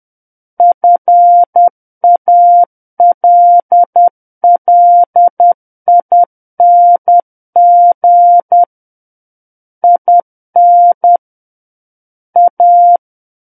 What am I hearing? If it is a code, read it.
Morse code, heard as a keyed tone, about 10 wpm, 700 Hz. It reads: FALLING IN A
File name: Book One / 01-Chapter 1 - 10 WPM